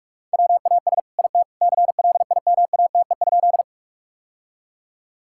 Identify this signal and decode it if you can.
Morse code, heard as a keyed tone, about 46 wpm, 700 Hz. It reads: WAS IT CLIMATE?
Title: Sentences from Top 1600 Words 45wpm